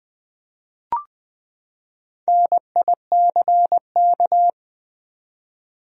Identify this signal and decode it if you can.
Morse code, heard as a keyed tone, about 20 wpm, 700 Hz. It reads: NICK
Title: USA Common Names - Max 4 Letters - 20wpm